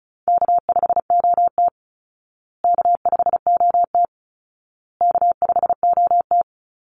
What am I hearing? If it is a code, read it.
Morse code, heard as a keyed tone, about 34 wpm, 700 Hz. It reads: K5OT K5OT K5OT
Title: Call Signs Repeated 2x - 35wpm